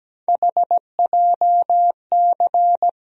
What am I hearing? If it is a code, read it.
Morse code, heard as a keyed tone, about 17 wpm, 700 Hz. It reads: HJC
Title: USA Call Sign Postfixes - 3 Letters 17wpm